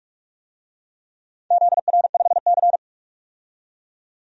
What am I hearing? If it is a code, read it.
Morse code, heard as a keyed tone, about 42 wpm, 700 Hz. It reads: ZR5C
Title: DXCC-100 Call Signs - 4 Characters - 45wpm